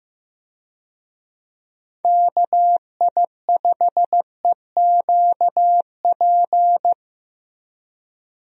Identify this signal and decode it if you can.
Morse code, heard as a keyed tone, about 15 wpm, 700 Hz. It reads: KI5EQP
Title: USA Call Signs - 6 Characters - Repeated 3x - 15wpm